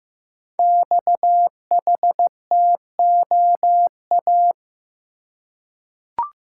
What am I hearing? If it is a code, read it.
Morse code, heard as a keyed tone, about 15 wpm, 700 Hz. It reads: XHTOA
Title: Code Groups - 5 Letters - 15wpm